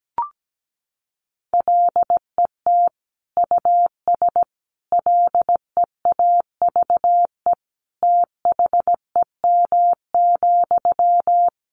Morse code, heard as a keyed tone, about 17 wpm, 700 Hz. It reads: LET US LEAVE THEM,
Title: Sentences from Top 400 Words 17wpm